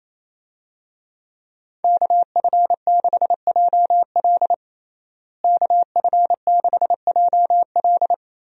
Morse code, heard as a keyed tone, about 28 wpm, 700 Hz. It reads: KF6JL KF6JL
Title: USA Call Signs - 5 Characters - Repeated 3x - 28wpm